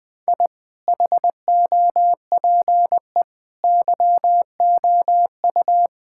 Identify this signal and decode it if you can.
Morse code, heard as a keyed tone, about 21 wpm, 700 Hz. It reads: I HOPE YOU
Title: Sentences from Top 1000 Words - Review Part 6 - 20wpm